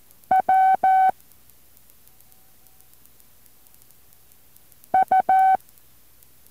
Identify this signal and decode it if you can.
Morse code, heard as a keyed tone, about 14 wpm, 760 Hz. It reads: W U